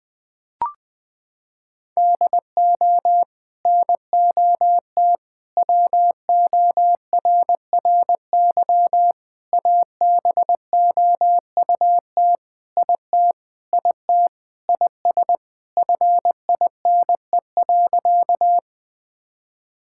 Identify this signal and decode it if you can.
Morse code, heard as a keyed tone, about 20 wpm, 700 Hz. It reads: DO NOT WORRY ABOUT IT IT IS FINE.
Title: Sentences with 5-Word Phrases - 20wpm